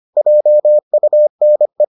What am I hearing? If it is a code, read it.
Morse code, heard as a keyed tone, about 25 wpm, 600 Hz. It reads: JUNE